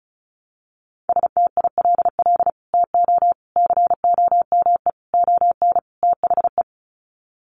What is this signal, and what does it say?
Morse code, heard as a keyed tone, about 34 wpm, 700 Hz. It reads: STILL TO COME ON THE
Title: Sets of 5 Words Encore2 35wpm